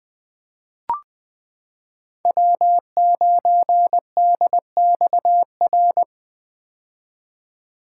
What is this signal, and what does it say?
Morse code, heard as a keyed tone, about 21 wpm, 700 Hz. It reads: W9DXR
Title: K1USN SST QSO Practice - 20wpm